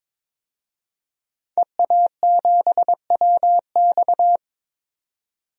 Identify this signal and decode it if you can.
Morse code, heard as a keyed tone, about 22 wpm, 700 Hz. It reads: EA7WX
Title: DXCC-100 Call Signs - 5 Characters - 22wpm